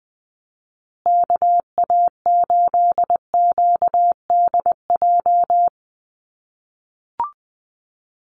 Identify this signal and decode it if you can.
Morse code, heard as a keyed tone, about 20 wpm, 700 Hz. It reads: KA8QDJ
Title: USA Call Signs Repeated 3x 20wpm